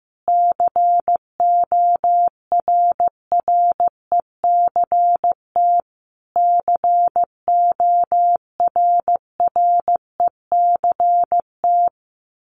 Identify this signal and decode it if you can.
Morse code, heard as a keyed tone, about 14 wpm, 700 Hz. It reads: CORRECT CORRECT
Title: Top 500 Words Repeated 15wpm